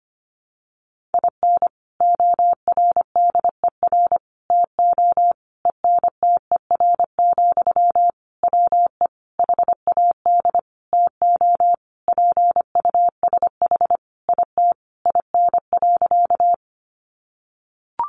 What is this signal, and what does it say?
Morse code, heard as a keyed tone, about 25 wpm, 700 Hz. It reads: IN ORDER TO ENTER, WE HAD TO PUSH IT IN.
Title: Sentences from Top 1300 Words 25wpm